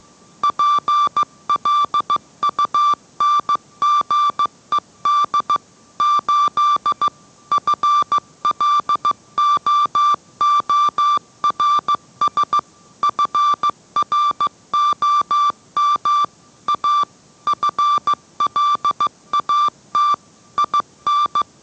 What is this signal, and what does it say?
Morse code, heard as a keyed tone, about 15 wpm, 1170 Hz. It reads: PLUNGED 8 FLOORS FROM A FLAT IN